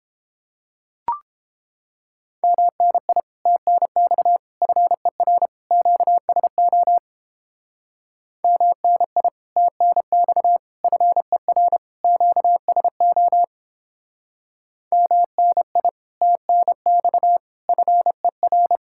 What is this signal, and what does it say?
Morse code, heard as a keyed tone, about 31 wpm, 700 Hz. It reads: MNI TNX FER QSO MNI TNX FER QSO MNI TNX FER